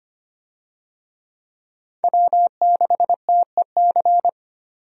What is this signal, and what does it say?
Morse code, heard as a keyed tone, about 25 wpm, 700 Hz. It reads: W6TEC
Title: USA Call Signs - 5 Characters - 25wpm